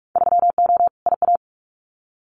Morse code, heard as a keyed tone, about 44 wpm, 700 Hz. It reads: 3O IA
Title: Winter Field Day Exchanges 45wpm